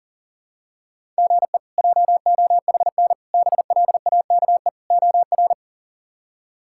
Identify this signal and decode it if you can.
Morse code, heard as a keyed tone, about 40 wpm, 700 Hz. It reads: GE JOHN BLAKE OR
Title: K1USN SST QSO Practice - 40wpm